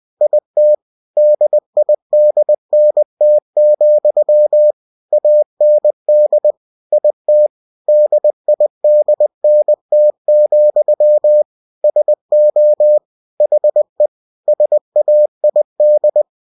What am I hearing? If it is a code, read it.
Morse code, heard as a keyed tone, about 20 wpm, 600 Hz. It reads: IT DIDNT, AND IT DIDNT, SO HE SAID
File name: cw-books / winnie-the-pooh / winnie-20wpm-page-0106